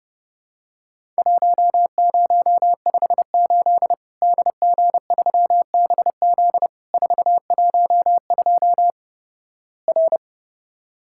Morse code, heard as a keyed tone, about 30 wpm, 700 Hz. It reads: 1058 DG3BZ 412 R
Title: Worked All Europe DX Contest - QTC Exchange - 30wpm